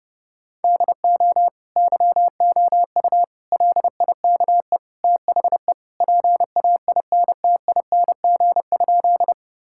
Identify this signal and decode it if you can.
Morse code, heard as a keyed tone, about 30 wpm, 700 Hz. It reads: DO YOU LIKE THE PAINTING?
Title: Sentences from Top 1000 Words - Review Part 6 - 30wpm